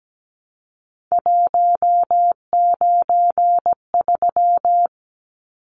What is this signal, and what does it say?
Morse code, heard as a keyed tone, about 17 wpm, 700 Hz. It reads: 193
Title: Numbers - 3-Digit 17wpm